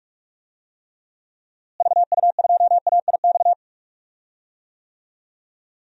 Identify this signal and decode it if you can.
Morse code, heard as a keyed tone, about 46 wpm, 700 Hz. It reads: VU2AIX
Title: DXCC-100 Call Signs - 6 Characters - Repeated 2x - 45wpm